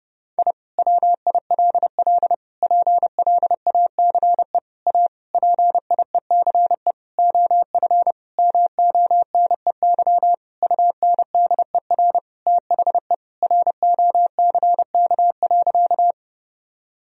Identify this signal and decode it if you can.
Morse code, heard as a keyed tone, about 32 wpm, 700 Hz. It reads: I WILL PLACE A PIECE OF MONEY UNDER THE ROCK.
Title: Sentences from Top 1000 Words - Review Part 2 - 30wpm